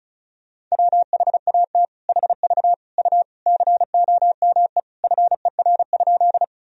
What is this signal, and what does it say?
Morse code, heard as a keyed tone, about 34 wpm, 700 Hz. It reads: WHAT HV U COME FER?
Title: Sentences from Top 100 Words and Abbreviations 35wpm